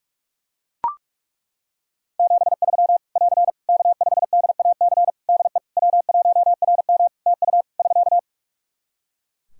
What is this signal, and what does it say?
Morse code, heard as a keyed tone, about 41 wpm, 700 Hz. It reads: 73 <AR> K5DAC DE W1RM TU <SK>
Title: Practice QSO - Exchange 4 - Ending - 45wpm